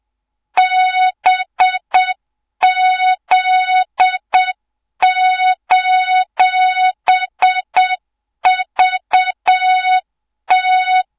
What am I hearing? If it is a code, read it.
Morse code, heard as a keyed tone, about 7 wpm, 3700 Hz, with noise. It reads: BZ:VT
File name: Morze-HA2MN / hang / 24 GYAKORLAT